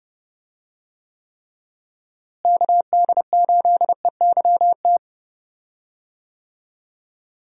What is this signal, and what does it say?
Morse code, heard as a keyed tone, about 30 wpm, 700 Hz. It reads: KD8EYT
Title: USA Call Signs - 6 Characters - Repeated 3x - 30wpm